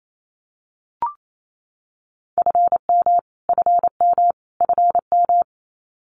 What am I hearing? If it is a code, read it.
Morse code, heard as a keyed tone, about 28 wpm, 700 Hz. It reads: FM FM FM
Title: Abbreviations Repeated 3x - 28wpm